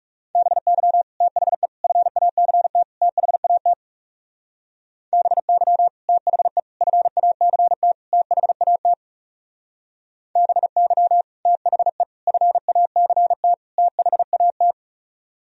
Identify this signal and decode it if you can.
Morse code, heard as a keyed tone, about 37 wpm, 690 Hz. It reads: BY THE FACT THAT BY THE FACT THAT BY THE FACT THAT
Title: Speed-Racing - Sets of 4 Words 30wpm